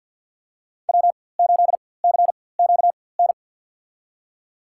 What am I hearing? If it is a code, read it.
Morse code, heard as a keyed tone, about 53 wpm, spaced as for 24 wpm, 700 Hz. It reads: W8CQN
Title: USA Call Signs - 5 Characters - 4x Character Spacing - 50wpm